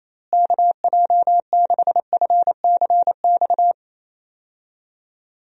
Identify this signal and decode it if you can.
Morse code, heard as a keyed tone, about 28 wpm, 700 Hz. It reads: KJ6FCX
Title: USA Call Signs - 6 Characters - Repeated 3x - 28wpm